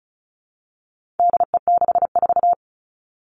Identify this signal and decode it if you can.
Morse code, heard as a keyed tone, about 34 wpm, 700 Hz. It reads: DE64
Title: All Cut Numbers - 4-Digit 35wpm